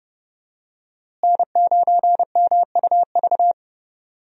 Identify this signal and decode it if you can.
Morse code, heard as a keyed tone, about 30 wpm, 700 Hz. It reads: N9MUV
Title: USA Call Signs - 5 Characters - Repeated 2x - 30wpm